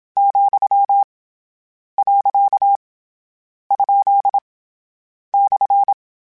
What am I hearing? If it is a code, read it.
Morse code, heard as a keyed tone, about 27 wpm, 800 Hz. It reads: , . ? /
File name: Short Runs / Short Runs-mp3s / Words @25